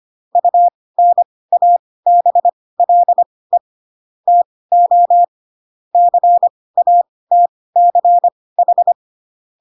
Morse code, heard as a keyed tone, about 25 wpm, spaced as for 18 wpm, 700 Hz. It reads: UNABLE TO CATCH
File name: Pinocchio ch3 12 25 ik1hge